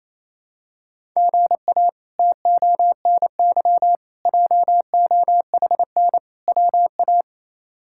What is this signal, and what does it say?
Morse code, heard as a keyed tone, about 28 wpm, 700 Hz. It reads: GA TONY JOHN WA
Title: K1USN SST QSO Practice - 28wpm